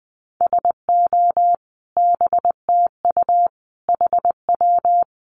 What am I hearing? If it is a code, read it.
Morse code, heard as a keyed tone, about 20 wpm, 700 Hz. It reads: SO BTU HW
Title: Common QSO Phrases 20wpm